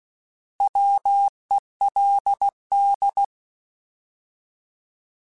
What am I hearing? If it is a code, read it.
Morse code, heard as a keyed tone, about 16 wpm, 800 Hz. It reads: WELD
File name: Alphabetical / Alphabetical mp3s / Ws @15